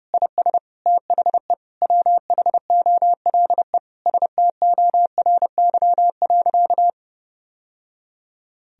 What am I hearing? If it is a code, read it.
Morse code, heard as a keyed tone, about 30 wpm, 700 Hz. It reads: IS THE WHOLE STORY.
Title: Sentences from Top 1000 Words - Review Part 1 - 30wpm